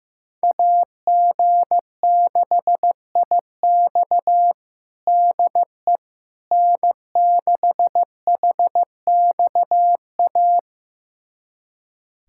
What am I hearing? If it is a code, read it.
Morse code, heard as a keyed tone, about 15 wpm, 700 Hz. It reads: AG6IX DE N6HXA
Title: USA Callsign DE Callsign - 5 Characters - 15wpm